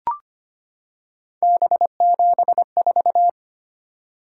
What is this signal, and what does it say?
Morse code, heard as a keyed tone, about 25 wpm, 700 Hz. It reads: B74
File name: Three Letter-Number Combinations - 25wpm